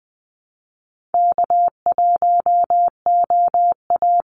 Morse code, heard as a keyed tone, about 20 wpm, 700 Hz. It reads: K1OA
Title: Call Signs Repeated 2x - 20wpm